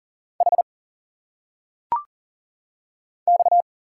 Morse code, heard as a keyed tone, about 40 wpm, 700 Hz. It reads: H X
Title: Single Letters - Phonetic Pronunciation 40wpm